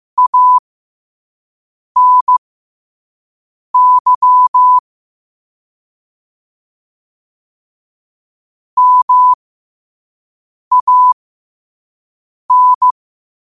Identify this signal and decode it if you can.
Morse code, heard as a keyed tone, about 15 wpm, spaced as for 5 wpm, 1000 Hz. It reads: ANY MAN
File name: qotdfiles / 5wpm